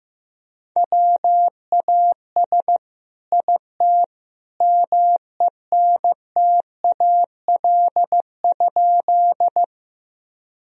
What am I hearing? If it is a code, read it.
Morse code, heard as a keyed tone, about 15 wpm, 700 Hz. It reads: WAS IT MENTAL?
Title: Sentences from Top 1000 Words - Review Part 4 - 15wpm